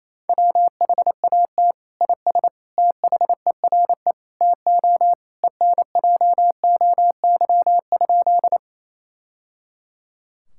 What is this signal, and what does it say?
Morse code, heard as a keyed tone, about 28 wpm, 700 Hz. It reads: WHAT IS THERE TO ENJOY?
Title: Sentences from Top 1000 Words - Review Part 5 - 28wpm